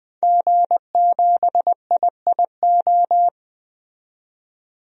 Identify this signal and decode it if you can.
Morse code, heard as a keyed tone, about 20 wpm, 700 Hz. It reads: G7IIO
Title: DXCC-100 Call Signs - 5 Characters - 20wpm